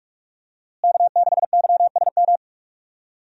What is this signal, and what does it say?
Morse code, heard as a keyed tone, about 42 wpm, 700 Hz. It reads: K6YSM
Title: USA Call Signs - 5 Characters - Repeated 3x - 45wpm